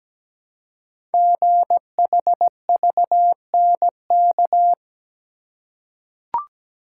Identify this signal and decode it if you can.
Morse code, heard as a keyed tone, about 17 wpm, 700 Hz. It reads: GHVNK
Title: Code Groups - 5 Letters - 17wpm